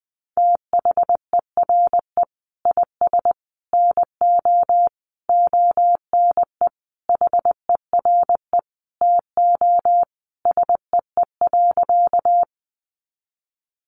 Morse code, heard as a keyed tone, about 20 wpm, 700 Hz. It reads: THERE IS NO ONE HERE TO SEE.
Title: Sentences from Top 200 Words 20wpm